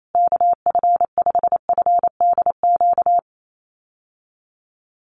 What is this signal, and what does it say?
Morse code, heard as a keyed tone, about 28 wpm, 700 Hz. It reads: KF5FDQ
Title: USA Call Signs Repeated 2x 28wpm